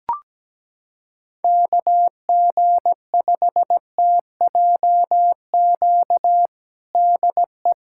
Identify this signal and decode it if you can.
Morse code, heard as a keyed tone, about 17 wpm, 700 Hz. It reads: KG5TJQ DE
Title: USA Callsign DE Callsign - 6 Characters - 17wpm